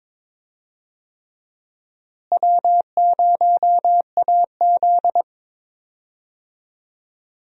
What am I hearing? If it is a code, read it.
Morse code, heard as a keyed tone, about 22 wpm, 700 Hz. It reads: W0AZ
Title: USA Call Signs - 4 Characters - Repeated 3x - 22wpm